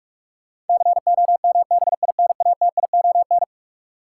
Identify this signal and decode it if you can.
Morse code, heard as a keyed tone, about 46 wpm, 700 Hz. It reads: COMBINATION
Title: Sentences from Top 1300 Words - New Word Drill Alt - 45wpm